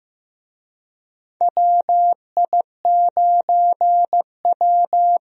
Morse code, heard as a keyed tone, about 15 wpm, 700 Hz. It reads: WI9W
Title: USA Call Signs - 4 Characters - Repeated 3x - 15wpm